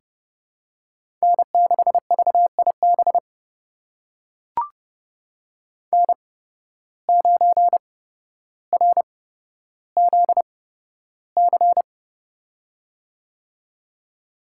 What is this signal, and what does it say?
Morse code, heard as a keyed tone, about 30 wpm, spaced as for 13 wpm, 700 Hz. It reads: N6VIB N9RZC
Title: USA Call Signs - 8x Character Spacing - 30wpm